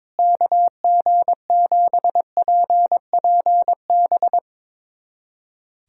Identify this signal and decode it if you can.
Morse code, heard as a keyed tone, about 22 wpm, 700 Hz. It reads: KG7PPB
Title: USA Callsign DE Callsign 22wpm